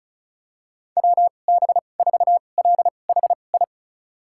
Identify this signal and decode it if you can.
Morse code, heard as a keyed tone, about 34 wpm, spaced as for 25 wpm, 700 Hz. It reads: WB4LHI
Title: USA Call Signs - 6 Characters - 2x Character Spacing - 35wpm